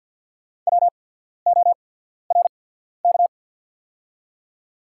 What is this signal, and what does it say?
Morse code, heard as a keyed tone, about 48 wpm, spaced as for 14 wpm, 690 Hz. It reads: WORK
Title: 8x Character Spacing - Top 300 Words - 50wpm